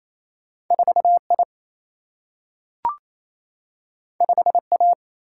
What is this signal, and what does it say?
Morse code, heard as a keyed tone, about 28 wpm, 700 Hz. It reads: 4I 5A
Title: Single-Letter Single-Number 28wpm